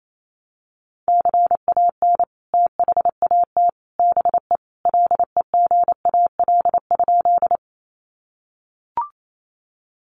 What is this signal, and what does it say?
Morse code, heard as a keyed tone, about 28 wpm, 700 Hz. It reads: CAN THAT BE LEGAL?
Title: Sentences from Top 1200 Words 28wpm